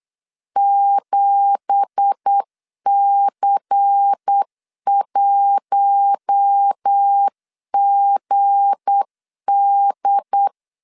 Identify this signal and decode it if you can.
Morse code, heard as a keyed tone, about 8 wpm, 800 Hz. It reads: 7C1GD
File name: CW / DET / 1 9 1500C S A 8wpm 5637